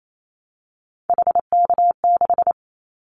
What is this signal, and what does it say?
Morse code, heard as a keyed tone, about 28 wpm, 700 Hz. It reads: HK6
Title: Callsign Prefixes for Top 100 DXCC - 28wpm